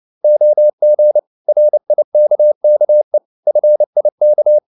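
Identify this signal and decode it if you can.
Morse code, heard as a keyed tone, about 29 wpm, 600 Hz. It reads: OG RIKKE FIK
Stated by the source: Fast (QRQ) Morse at 29 wpm, from CW med Gnister.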